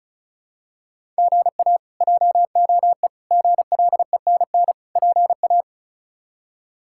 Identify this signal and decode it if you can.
Morse code, heard as a keyed tone, about 36 wpm, 700 Hz. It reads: GA JOE GLENN PA
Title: K1USN SST QSO Practice - 35wpm